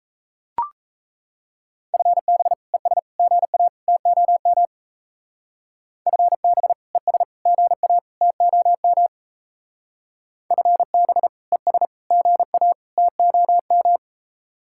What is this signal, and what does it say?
Morse code, heard as a keyed tone, about 35 wpm, 700 Hz. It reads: FB ES GA TOM FB ES GA TOM FB ES GA TOM
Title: Speed-Racing Common QSO Phrases - 28wpm